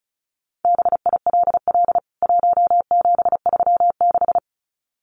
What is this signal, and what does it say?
Morse code, heard as a keyed tone, about 34 wpm, 700 Hz. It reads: BILL 1736
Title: CWT Exchange 35wpm